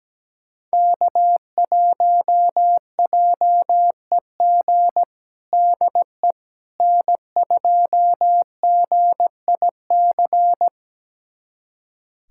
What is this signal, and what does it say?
Morse code, heard as a keyed tone, about 17 wpm, 700 Hz. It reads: K1JEG DE N2GIC
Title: USA Callsign DE Callsign 17wpm